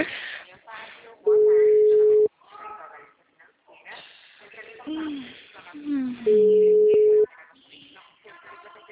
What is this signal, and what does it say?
Ringback tone (Europe, 425 Hz), with speech.